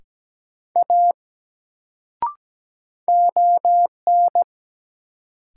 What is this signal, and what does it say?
Morse code, heard as a keyed tone, about 17 wpm, 700 Hz. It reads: A ON
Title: Sentences from Top 100 Words - New Word Drill - 17wpm